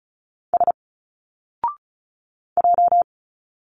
Morse code, heard as a keyed tone, about 34 wpm, 700 Hz. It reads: S J
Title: Single Letters 35wpm